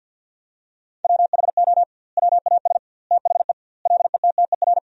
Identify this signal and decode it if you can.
Morse code, heard as a keyed tone, about 50 wpm, 700 Hz. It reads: WHO WAS THE LETTER
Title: 2x Word-Spacing - Sentences from Top 300 Words - 50wpm